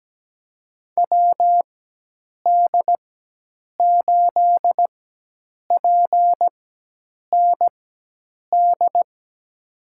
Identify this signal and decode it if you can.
Morse code, heard as a keyed tone, about 17 wpm, spaced as for 8 wpm, 700 Hz. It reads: WD8PND
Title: USA Call Signs - 6 Characters - 4x Character Spacing - 17wpm